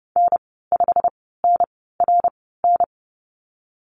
Morse code, heard as a keyed tone, about 30 wpm, spaced as for 17 wpm, 700 Hz. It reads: N5NRN
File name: Lesson 19g - Letter w - Call Signs 30wpm